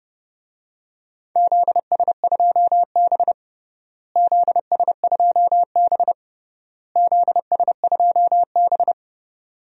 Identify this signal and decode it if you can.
Morse code, heard as a keyed tone, about 30 wpm, 700 Hz. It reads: ZS2B ZS2B ZS2B
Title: DXCC-100 Call Signs - 4 Characters - Repeated 3x - 30wpm